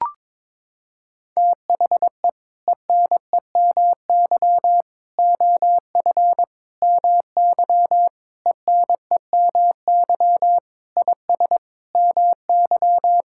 Morse code, heard as a keyed tone, about 22 wpm, 700 Hz. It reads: THE ENEMY OF MY ENEMY IS MY
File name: practice / 22wpm / Sets of 8 Words - 22wpm